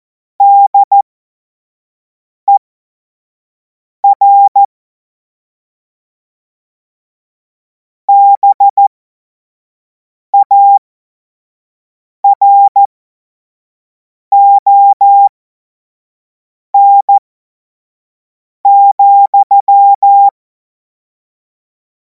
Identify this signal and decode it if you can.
Morse code, heard as a keyed tone, about 14 wpm, spaced as for 5 wpm, 800 Hz. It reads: DER BARON,
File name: Morsekurs / Witze / Witz 24. März 2026 25 F70